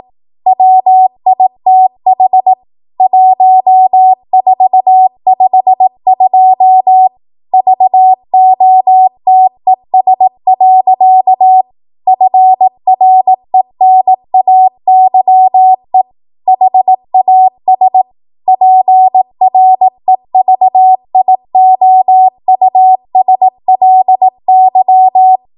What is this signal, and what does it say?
Morse code, heard as a keyed tone, about 18 wpm, 750 Hz. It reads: WITH 1452 VOTES. FRENAYE HAS PREVIOUSLY